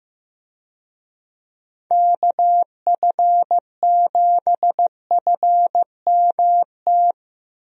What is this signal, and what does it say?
Morse code, heard as a keyed tone, about 15 wpm, 700 Hz. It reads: KF7FMT
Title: USA Call Signs - 6 Characters - 15wpm